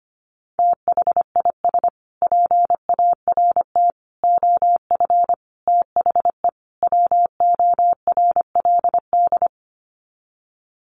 Morse code, heard as a keyed tone, about 25 wpm, 700 Hz. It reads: THIS PART OF THE WORLD
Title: Sets of 5 Words Encore2 25wpm